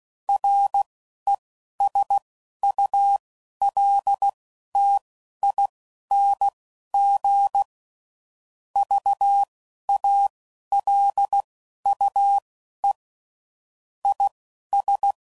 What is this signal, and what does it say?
Morse code, heard as a keyed tone, about 16 wpm, spaced as for 11 wpm, 800 Hz. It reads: RESULTING VALUE IS
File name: CW Code Groups @10WPM / Technical Subjects / XC